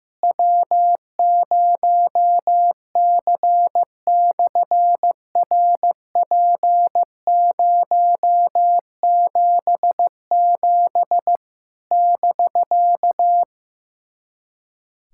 Morse code, heard as a keyed tone, about 15 wpm, 700 Hz. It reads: W0C/RP077 <BK>
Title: SOTA QSO Practice 15wpm